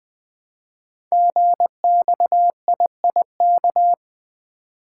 Morse code, heard as a keyed tone, about 20 wpm, 700 Hz. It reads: GXIIK
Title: Code Groups - 5 Letters - 20wpm